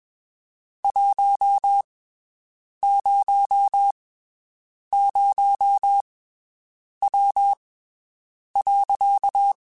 Morse code, heard as a keyed tone, about 21 wpm, spaced as for 7 wpm, 800 Hz. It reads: 100W.
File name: (spaced) / QSOs Spaced-mp3s / WD0FBT 1 @20 sp